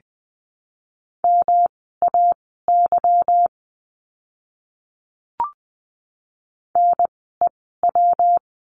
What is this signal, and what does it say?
Morse code, heard as a keyed tone, about 20 wpm, spaced as for 15 wpm, 700 Hz. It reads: MAY NEW
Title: 160 Lesson 26f - Letter g - Words 20wpm